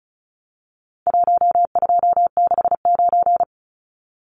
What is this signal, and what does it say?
Morse code, heard as a keyed tone, about 34 wpm, 700 Hz. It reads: 1269
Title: Numbers - Contest Style - 4-Digit 35wpm